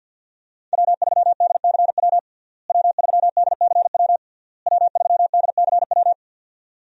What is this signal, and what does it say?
Morse code, heard as a keyed tone, about 51 wpm, 700 Hz. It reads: W3DCW W3DCW W3DCW
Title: USA Call Signs - 5 Characters - Repeated 3x - 50wpm